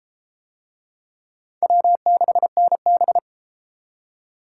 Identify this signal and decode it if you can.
Morse code, heard as a keyed tone, about 33 wpm, 700 Hz. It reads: W6NB